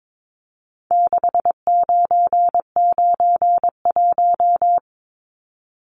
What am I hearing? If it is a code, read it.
Morse code, heard as a keyed tone, about 22 wpm, 700 Hz. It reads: 6991
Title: Numbers - 4-Digit 22wpm